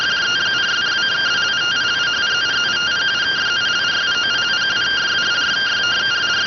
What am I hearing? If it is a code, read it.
Two-tone teleprinter signal, about 170 Hz shift.